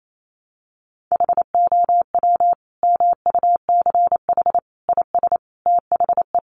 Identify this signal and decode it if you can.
Morse code, heard as a keyed tone, about 28 wpm, 700 Hz. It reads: HOW MUCH IS THE
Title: Sentences from Top 1200 Words 28wpm